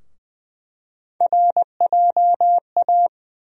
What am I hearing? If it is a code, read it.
Morse code, heard as a keyed tone, about 21 wpm, 700 Hz. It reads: RJA
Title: USA Call Sign Postfixes - 3 Letters 20wpm